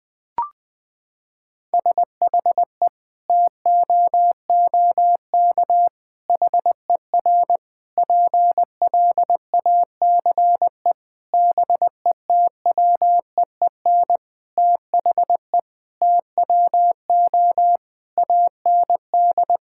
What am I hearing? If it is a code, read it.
Morse code, heard as a keyed tone, about 20 wpm, 700 Hz. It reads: SHE TOOK HER PLACE BETWEEN THE TWO AND
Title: Sentences from Top 1000 Words - Review Part 5 - 20wpm